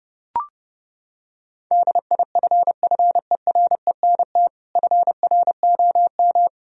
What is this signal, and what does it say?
Morse code, heard as a keyed tone, about 30 wpm, 700 Hz. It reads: DIFFERENT FROM
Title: Sentences from Top 1000 Words - Review Part 3 - 30wpm